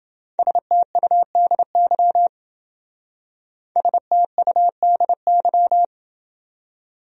Morse code, heard as a keyed tone, about 28 wpm, 700 Hz. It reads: STUDY STUDY